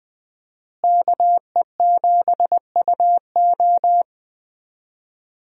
Morse code, heard as a keyed tone, about 20 wpm, 700 Hz. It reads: KE7UO
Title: USA Call Signs - 5 Characters - Repeated 3x - 20wpm